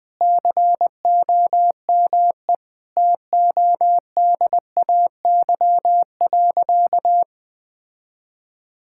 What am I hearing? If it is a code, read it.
Morse code, heard as a keyed tone, about 20 wpm, 700 Hz. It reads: COME TODAY.
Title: Sentences with 5-Word Phrases - Spoken Beginning - 20wpm